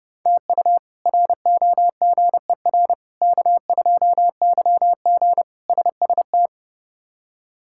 Morse code, heard as a keyed tone, about 30 wpm, 700 Hz. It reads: TU ROGER K2YG SST
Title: K1USN SST QSO Practice - 30wpm